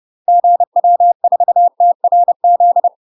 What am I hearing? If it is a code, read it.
Morse code, heard as a keyed tone, about 30 wpm, 700 Hz. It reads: GW4TRZ
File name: UK MASTER QSOs TRAINING / 30WPM / UK MASTER QSOs 0046